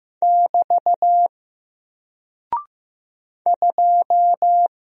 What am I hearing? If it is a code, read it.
Morse code, heard as a keyed tone, about 15 wpm, 700 Hz. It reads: <BT> 2
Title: Single Character (Letter:Number:Punctuation:Prosign) 15wpm